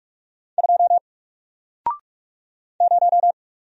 Morse code, heard as a keyed tone, about 44 wpm, 700 Hz. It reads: 2 0
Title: Single Character (Letter:Number:Punctuation:Prosign) 45wpm